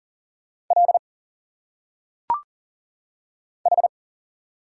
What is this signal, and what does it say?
Morse code, heard as a keyed tone, about 40 wpm, 700 Hz. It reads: L H
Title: Single Letters - Phonetic Pronunciation 40wpm